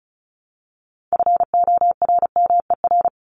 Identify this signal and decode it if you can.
Morse code, heard as a keyed tone, about 34 wpm, 700 Hz. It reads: FORMER
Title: LY Suffix - Top 100 Words with Base Words 35wpm